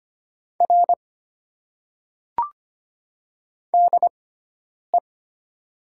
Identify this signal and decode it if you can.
Morse code, heard as a keyed tone, about 25 wpm, spaced as for 9 wpm, 690 Hz. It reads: R DE
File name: Top 90 QSO Elements - 6x Character Spacing - 25wpm